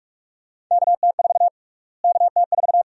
Morse code, heard as a keyed tone, about 46 wpm, 700 Hz. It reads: KT4 KT4
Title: USA Call Sign Prefixes - Repeated 2x - 45wpm